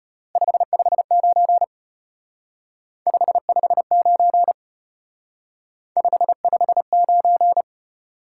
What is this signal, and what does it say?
Morse code, heard as a keyed tone, about 32 wpm, 700 Hz. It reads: 559 559 559